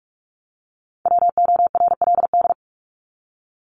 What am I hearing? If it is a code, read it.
Morse code, heard as a keyed tone, about 44 wpm, 700 Hz. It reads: WORLD
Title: Top 200 Words 45wpm